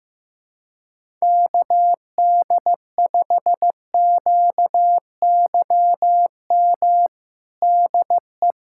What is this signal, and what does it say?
Morse code, heard as a keyed tone, about 16 wpm, 700 Hz. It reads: KD5QYM DE
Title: USA Callsign DE Callsign - 6 Characters - 15wpm